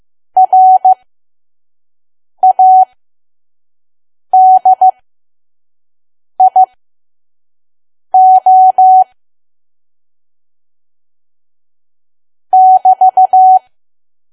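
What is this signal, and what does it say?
Morse code, heard as a keyed tone, about 15 wpm, spaced as for 5 wpm, 750 Hz. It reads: RADIO <BT>
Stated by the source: Morse code practice at 5 WPM from KQ2P.COM.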